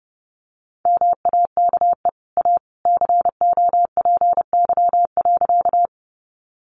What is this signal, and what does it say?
Morse code, heard as a keyed tone, about 30 wpm, 700 Hz. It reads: MAKE A COPY.
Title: Sentences from Top 1400 Words 30wpm